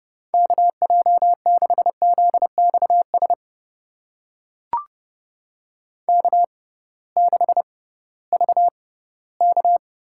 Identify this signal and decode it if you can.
Morse code, heard as a keyed tone, about 30 wpm, spaced as for 11 wpm, 700 Hz. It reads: KJ6ZX SK6VK
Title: USA Call Signs - 6x Character Spacing - 30wpm